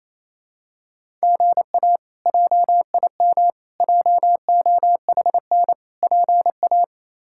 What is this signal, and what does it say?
Morse code, heard as a keyed tone, about 28 wpm, 700 Hz. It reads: GA JIM JOHN PA
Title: K1USN SST QSO Practice - 28wpm